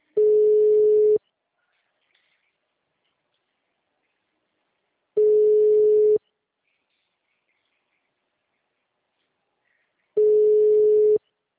Ringback tone (Europe, 425 Hz).